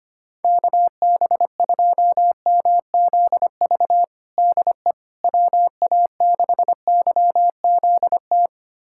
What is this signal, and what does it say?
Morse code, heard as a keyed tone, about 25 wpm, 700 Hz. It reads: KB2MZV DE WA6YZT
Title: USA Callsign DE Callsign - 6 Characters - 25wpm